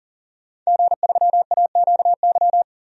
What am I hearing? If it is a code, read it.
Morse code, heard as a keyed tone, about 40 wpm, 700 Hz. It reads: G3AQY
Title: DXCC-100 Call Signs - 5 Characters - 40wpm